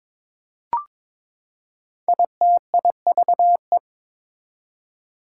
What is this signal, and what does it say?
Morse code, heard as a keyed tone, about 22 wpm, 700 Hz. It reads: ITIVE
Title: Prefixes and Suffixes 22wpm